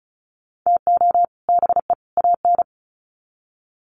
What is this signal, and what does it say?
Morse code, heard as a keyed tone, about 34 wpm, 700 Hz. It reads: TO BE AN
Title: Sets of 3 Words 35wpm